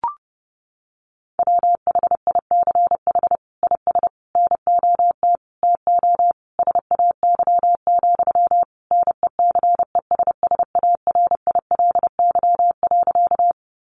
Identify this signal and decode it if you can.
Morse code, heard as a keyed tone, about 28 wpm, 700 Hz. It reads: WHICH IS NOT TO SAY, NECESSARILY.
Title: Sentences from Top 1400 Words 30wpm